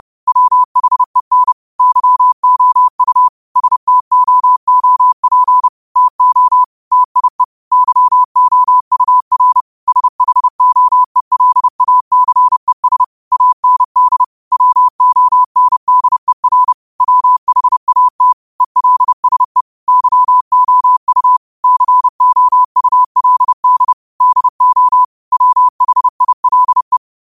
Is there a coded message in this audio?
Yes: Morse code from 0.3 to 27.0 s, heard as a keyed tone, about 30 wpm, 1000 Hz. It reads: WHEN YOU STOOP TO TIE YOUR SHOELACES AND WONDER WHAT ELSE YOU COULD DO WHILE